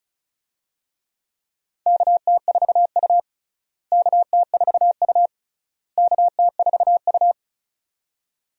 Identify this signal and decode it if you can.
Morse code, heard as a keyed tone, about 36 wpm, 700 Hz. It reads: KT4U KT4U KT4U
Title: USA Call Signs - 4 Characters - Repeated 3x - 35wpm